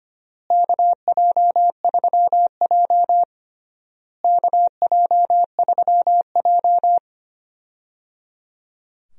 Morse code, heard as a keyed tone, about 25 wpm, 700 Hz. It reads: KJ3J KJ3J
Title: USA Call Signs - 4 Characters - Repeated 3x - 25wpm